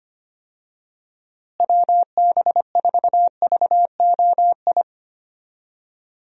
Morse code, heard as a keyed tone, about 25 wpm, 700 Hz. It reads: WB4VOI
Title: USA Call Signs - 6 Characters - Repeated 3x - 25wpm